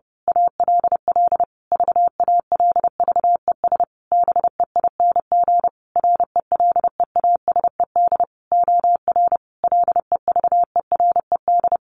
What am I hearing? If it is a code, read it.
Morse code, heard as a keyed tone, about 30 wpm, 700 Hz. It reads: ALL VALVES BEING RELEASED OR LEVERED
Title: A Sound Of Thunder 30wpm